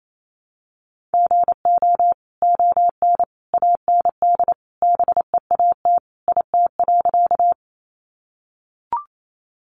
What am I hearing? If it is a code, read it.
Morse code, heard as a keyed tone, about 28 wpm, 700 Hz. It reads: GO ON AND BEAT IT.
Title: Sentences from Top 1300 Words 28wpm